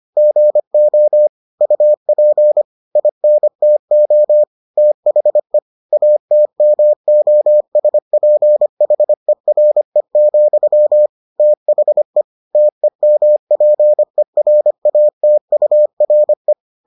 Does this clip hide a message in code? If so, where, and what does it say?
Yes: Morse code from 0.2 to 16.5 s, heard as a keyed tone, about 25 wpm, 600 Hz. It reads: GO UP INTO THE ATMOSPHERE, THE TEMPERATURE